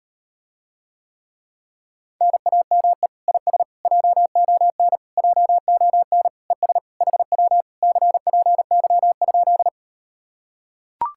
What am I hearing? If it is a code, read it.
Morse code, heard as a keyed tone, about 38 wpm, 700 Hz. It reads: NAME IS JON JON ES HW CPY?